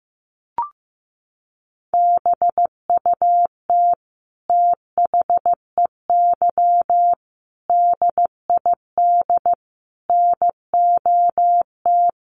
Morse code, heard as a keyed tone, about 15 wpm, 700 Hz. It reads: BUT THEY DID NOT
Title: Sentences from Top 1300 Words 15wpm